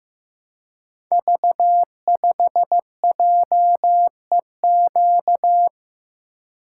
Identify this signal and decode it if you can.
Morse code, heard as a keyed tone, about 15 wpm, 700 Hz. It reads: V5JEQ
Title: Code Groups - 5 Letter:Numbers - 15wpm